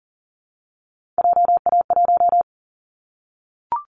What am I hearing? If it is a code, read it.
Morse code, heard as a keyed tone, about 40 wpm, 700 Hz. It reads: JA1
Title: Callsign Prefixes for Top 50 DXCC - 40wpm